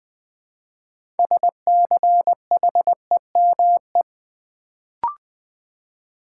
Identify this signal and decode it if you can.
Morse code, heard as a keyed tone, about 20 wpm, 700 Hz. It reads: SCHEME
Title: Build A Word - 2 Letters At A Time - 6-Letter Words - 20wpm